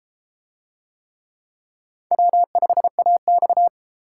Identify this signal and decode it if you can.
Morse code, heard as a keyed tone, about 32 wpm, 700 Hz. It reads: W5AX